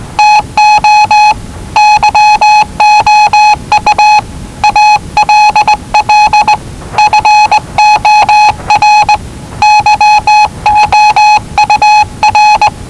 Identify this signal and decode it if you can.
Morse code, heard as a keyed tone, about 19 wpm, 840 Hz. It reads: TO YOU ALL FOR YOUR
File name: WellingtonRadioZlwCloseDownOnHfAndMf / 01-Zlw 4277khz 417.5 qrt until 0925z